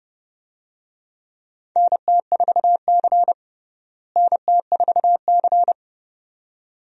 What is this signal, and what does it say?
Morse code, heard as a keyed tone, about 31 wpm, 700 Hz. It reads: NT4C NT4C
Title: USA Call Signs - 4 Characters - Repeated 2x - 30wpm